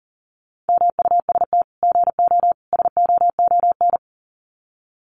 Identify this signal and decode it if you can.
Morse code, heard as a keyed tone, about 40 wpm, 700 Hz. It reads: MUST GO SOON
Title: Common QSO Phrases 40wpm